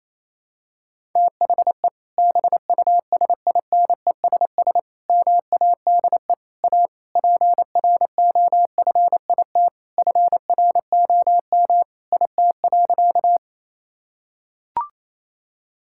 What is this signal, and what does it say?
Morse code, heard as a keyed tone, about 28 wpm, 700 Hz. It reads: THE BUSINESS MADE A PROFIT FROM IT.
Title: Sentences from Top 1000 Words - Review Part 5 - 28wpm